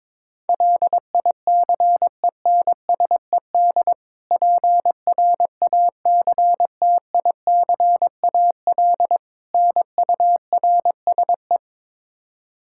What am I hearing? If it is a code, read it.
Morse code, heard as a keyed tone, about 21 wpm, 700 Hz. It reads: LICENSED PRACTICAL NURSE
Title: Practice QSO - Exchange 3 - More Info - 22wpm